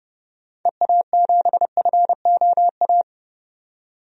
Morse code, heard as a keyed tone, about 30 wpm, 700 Hz. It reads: EA7FOA
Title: DXCC-100 Call Signs - 6 Characters - 30wpm